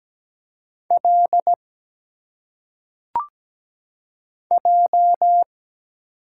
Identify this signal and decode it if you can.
Morse code, heard as a keyed tone, about 17 wpm, 700 Hz. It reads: L J
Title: Single Letters - Phonetic Pronunciation 17wpm